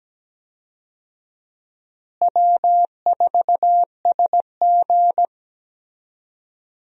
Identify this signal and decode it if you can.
Morse code, heard as a keyed tone, about 17 wpm, 700 Hz. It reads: W4SG
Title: USA Call Signs - 4 Characters - Repeated 2x - 17wpm